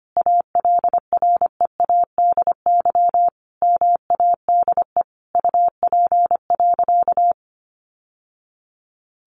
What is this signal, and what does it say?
Morse code, heard as a keyed tone, about 25 wpm, 700 Hz. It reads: ALREADY MADE UP.
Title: Sentences from Top 700 Words 25wpm